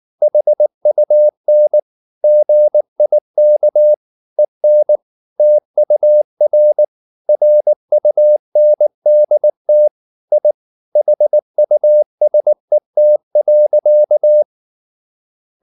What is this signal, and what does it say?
Morse code, heard as a keyed tone, about 19 wpm, 600 Hz. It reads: HUN GIK EN TUR RUNDT I HUSET.
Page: Ferie 19 wpm